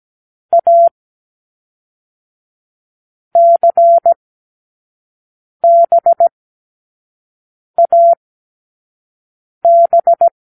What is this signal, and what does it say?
Morse code, heard as a keyed tone, about 17 wpm, 680 Hz. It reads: A C B A B
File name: k7qo / Lesson6(A-D)